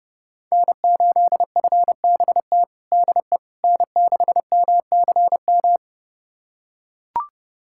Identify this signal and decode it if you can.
Morse code, heard as a keyed tone, about 30 wpm, 700 Hz. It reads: N8FBT DE N6MCM
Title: USA Callsign DE Callsign - 5 Characters - 30wpm